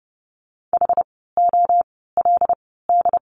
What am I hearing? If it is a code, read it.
Morse code, heard as a keyed tone, about 30 wpm, spaced as for 17 wpm, 700 Hz. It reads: HOLD
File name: course / 30:10wpm / 115 Lesson 19f - Letter w - Words 30wpm